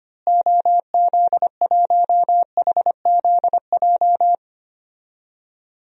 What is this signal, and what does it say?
Morse code, heard as a keyed tone, about 25 wpm, 700 Hz. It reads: OZ1HZJ
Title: DXCC-100 Call Signs - 6 Characters - Repeated 3x - 25wpm